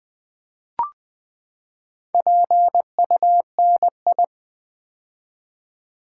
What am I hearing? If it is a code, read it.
Morse code, heard as a keyed tone, about 20 wpm, 700 Hz. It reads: PUNI
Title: Build A Word - 4 Letters At A Time - 10-Letter Words - 20wpm